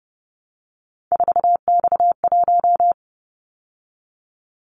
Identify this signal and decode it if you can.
Morse code, heard as a keyed tone, about 30 wpm, 700 Hz. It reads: 4X1
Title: Callsign Prefixes for Top 50 DXCC - 30wpm